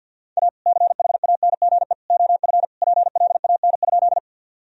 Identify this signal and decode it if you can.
Morse code, heard as a keyed tone, about 50 wpm, 700 Hz. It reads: A CHANGE OF PLAN?
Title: Sentences from Top 1000 Words - Review Part 2 - 50wpm